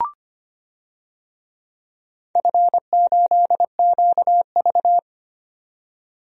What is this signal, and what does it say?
Morse code, heard as a keyed tone, about 25 wpm, 700 Hz. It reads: F8QV
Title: DXCC-100 Call Signs - 4 Characters - 25wpm